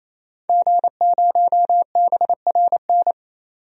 Morse code, heard as a keyed tone, about 28 wpm, 700 Hz. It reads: G0BRN
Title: DXCC-100 Call Signs - 5 Characters - 28wpm